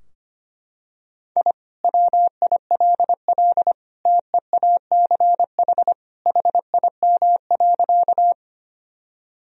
Morse code, heard as a keyed tone, about 25 wpm, 700 Hz. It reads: I WILL TEACH HIM.
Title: Sentences from Top 1000 Words - Review Part 2 - 25wpm